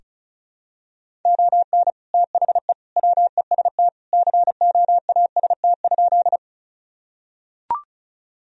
Morse code, heard as a keyed tone, about 34 wpm, 700 Hz. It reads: ON THE WEST COAST?
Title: Sentences from Top 1000 Words - Review Part 4 - 35wpm